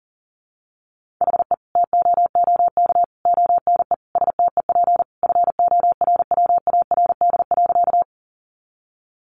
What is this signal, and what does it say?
Morse code, heard as a keyed tone, about 40 wpm, 700 Hz. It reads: HE TOOK ONE STEP FORWARD.
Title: Sentences from Top 1300 Words 40wpm